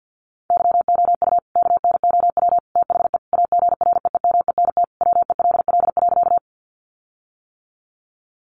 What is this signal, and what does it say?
Morse code, heard as a keyed tone, about 50 wpm, 700 Hz. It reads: YOU KNOW THE AGREEMENT WELL.
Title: Sentences from Top 1500 Words 50wpm